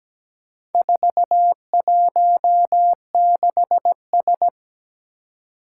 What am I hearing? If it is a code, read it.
Morse code, heard as a keyed tone, about 17 wpm, 700 Hz. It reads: 416S
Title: Four Challenging Letter-Number Combinations 17wpm